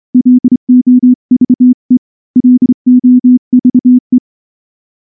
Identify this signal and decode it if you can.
Morse code, heard as a keyed tone, about 25 wpm, 250 Hz. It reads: LOVE LOVE
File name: morse / morse-training-25WPM